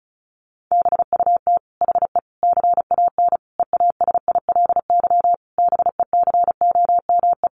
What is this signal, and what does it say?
Morse code, heard as a keyed tone, about 34 wpm, 700 Hz. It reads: BUT HE CAN EASILY BECOME
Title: Sentences from Top 900 Words 35wpm